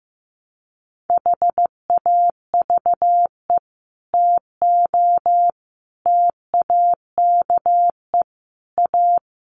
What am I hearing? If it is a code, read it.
Morse code, heard as a keyed tone, about 15 wpm, 700 Hz. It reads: HAVE TO TAKE A
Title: Sets of 5 Words Encore 15wpm